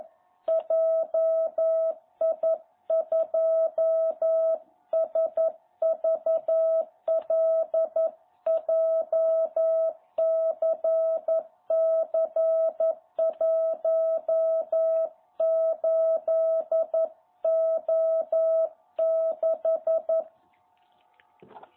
Morse code, heard as a keyed tone, about 11 wpm, 660 Hz. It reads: JI2SVL JCC18O6
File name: jh8cbh / 2021cw